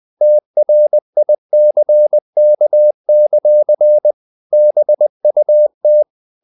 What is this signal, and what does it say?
Morse code, heard as a keyed tone, about 20 wpm, 600 Hz. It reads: TRICK; BUT
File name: the old man and the sea part 2 20wpm